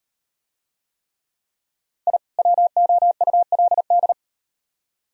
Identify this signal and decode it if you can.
Morse code, heard as a keyed tone, about 38 wpm, 700 Hz. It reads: I WOULD